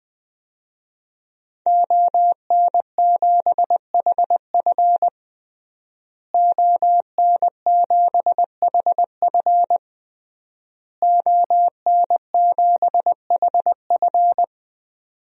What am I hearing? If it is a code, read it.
Morse code, heard as a keyed tone, about 20 wpm, 700 Hz. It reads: ON7HF ON7HF ON7HF
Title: DXCC-100 Call Signs - 5 Characters - Repeated 3x - 20wpm